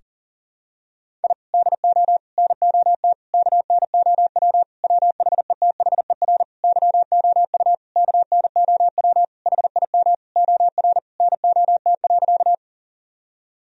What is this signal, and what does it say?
Morse code, heard as a keyed tone, about 40 wpm, 700 Hz. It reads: I DO NOT KNOW WHETHER YOU KNOW HIM OR NOT.
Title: Sentences from Top 1000 Words - Review Part 3 - 40wpm